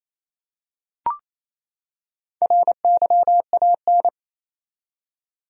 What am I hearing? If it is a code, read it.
Morse code, heard as a keyed tone, about 28 wpm, 700 Hz. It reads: RYAN
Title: USA Common Names - 28wpm